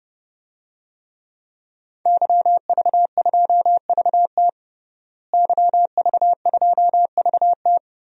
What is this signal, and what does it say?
Morse code, heard as a keyed tone, about 32 wpm, 700 Hz. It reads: YV2VT YV2VT
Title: DXCC-100 Call Signs - 5 Characters - Repeated 3x - 30wpm